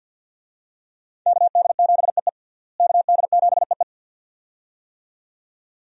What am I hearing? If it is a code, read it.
Morse code, heard as a keyed tone, about 53 wpm, 700 Hz. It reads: KD7EE KD7EE
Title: USA Call Signs - 5 Characters - Repeated 2x - 50wpm